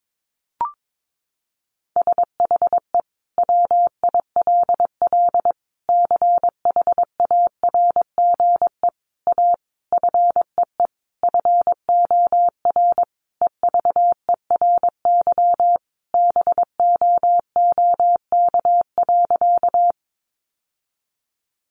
Morse code, heard as a keyed tone, about 22 wpm, 700 Hz. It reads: SHE WILL CHARGE A FEE FOR EVERY BOOK.
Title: Sentences from Top 1500 Words 22wpm